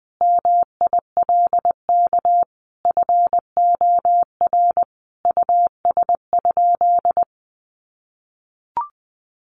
Morse code, heard as a keyed tone, about 20 wpm, 700 Hz. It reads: MILK FOR US?
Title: Sentences from Top 1500 Words 20wpm